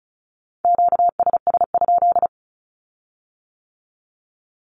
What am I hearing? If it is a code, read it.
Morse code, heard as a keyed tone, about 36 wpm, 700 Hz. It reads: QSS?
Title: Q-Codes with Spoken Meaning - 35wpm